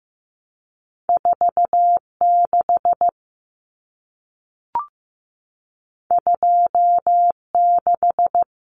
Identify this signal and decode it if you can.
Morse code, heard as a keyed tone, about 15 wpm, 700 Hz. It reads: 46 26
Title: Common Cut Numbers - 2-Digit 15wpm